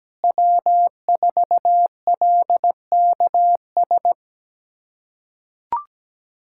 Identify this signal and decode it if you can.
Morse code, heard as a keyed tone, about 17 wpm, 700 Hz. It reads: W4LKS
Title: USA Callsign DE Callsign 17wpm